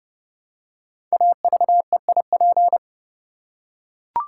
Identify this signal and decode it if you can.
Morse code, heard as a keyed tone, about 30 wpm, 700 Hz. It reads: AVEIP
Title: Code Groups - 5 Letters - 30wpm